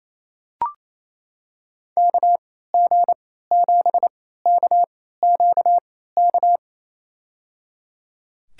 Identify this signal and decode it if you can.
Morse code, heard as a keyed tone, about 28 wpm, spaced as for 16 wpm, 700 Hz. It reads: KG7KQK
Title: USA Call Signs - 6 Characters - 3x Character Spacing - 28wpm